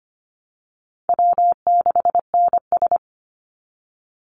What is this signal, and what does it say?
Morse code, heard as a keyed tone, about 25 wpm, 700 Hz. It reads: W6NS
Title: Call Signs Repeated 3x - 25wpm